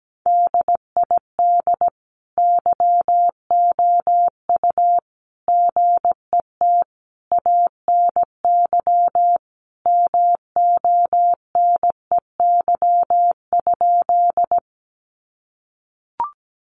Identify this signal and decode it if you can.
Morse code, heard as a keyed tone, about 17 wpm, 700 Hz. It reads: DID YOU GET ANY MONEY?
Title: Sentences from Top 400 Words 17wpm